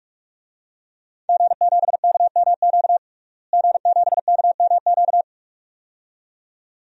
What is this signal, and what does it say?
Morse code, heard as a keyed tone, about 41 wpm, 700 Hz. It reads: G7KMQ G7KMQ
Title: DXCC-100 Call Signs - 5 Characters - Repeated 2x - 45wpm